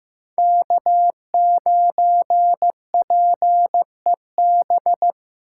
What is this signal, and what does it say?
Morse code, heard as a keyed tone, about 15 wpm, 700 Hz. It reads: K9PEB
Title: USA Call Signs - 5 Characters - 8x Character Spacing - 15wpm